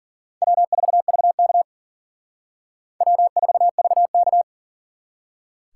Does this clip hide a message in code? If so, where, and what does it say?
Yes: Morse code from 0.4 to 4.4 s, heard as a keyed tone, about 50 wpm, 700 Hz. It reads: W4VK W4VK